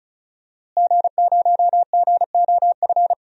Morse code, heard as a keyed tone, about 36 wpm, 700 Hz. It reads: G0GOF
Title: DXCC-100 Call Signs - 5 Characters - Repeated 2x - 35wpm